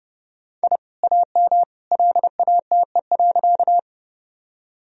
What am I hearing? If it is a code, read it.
Morse code, heard as a keyed tone, about 31 wpm, 700 Hz. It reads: I AM LATE.
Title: Sentences from Top 1000 Words - Review Part 2 - 30wpm